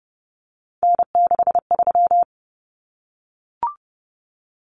Morse code, heard as a keyed tone, about 30 wpm, 700 Hz. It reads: N63
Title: Common Cut Numbers - 3-Digit 30wpm